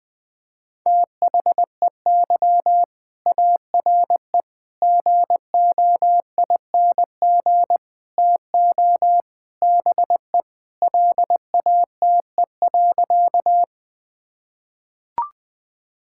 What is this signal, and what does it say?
Morse code, heard as a keyed tone, about 20 wpm, 700 Hz. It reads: THEY ARE GOING TO BE LATE.
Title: Sentences with 4-Word Phrases - 20wpm